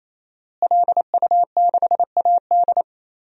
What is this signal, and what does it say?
Morse code, heard as a keyed tone, about 28 wpm, 700 Hz. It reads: LU6AD
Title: DXCC-100 Call Signs - 5 Characters - Repeated 3x - 28wpm